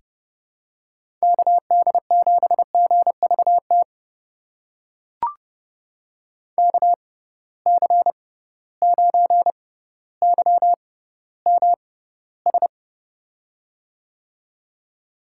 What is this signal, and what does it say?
Morse code, heard as a keyed tone, about 30 wpm, spaced as for 15 wpm, 700 Hz. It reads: KD7GVT KC9YMS
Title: USA Call Signs - 6x Character Spacing - 30wpm